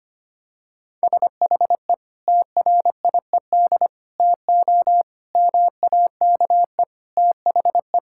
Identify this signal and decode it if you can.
Morse code, heard as a keyed tone, about 25 wpm, 700 Hz. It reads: SHE TRIED TO MAKE THE
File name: practice / 25wpm / Sentences with 3-Word Phrases - 25wpm